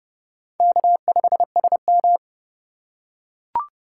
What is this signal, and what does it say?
Morse code, heard as a keyed tone, about 31 wpm, 700 Hz. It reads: K5SM